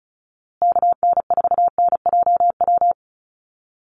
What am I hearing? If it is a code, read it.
Morse code, heard as a keyed tone, about 34 wpm, 700 Hz. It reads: KN4NJW
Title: USA Call Signs Repeated 2x 35wpm